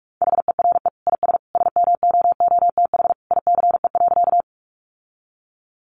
Morse code, heard as a keyed tone, about 44 wpm, 700 Hz. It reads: HERE IS SMOOTH ICE.
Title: Sentences from Top 1400 Words 45wpm